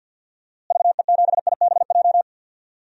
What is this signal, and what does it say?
Morse code, heard as a keyed tone, about 52 wpm, 700 Hz. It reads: VE7IBJ
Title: DXCC-100 Call Signs - 6 Characters - Repeated 3x - 50wpm